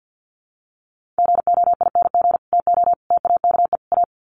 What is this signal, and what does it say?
Morse code, heard as a keyed tone, about 50 wpm, 700 Hz. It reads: GOING TO TAKE A
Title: Sets of 4 Words 50wpm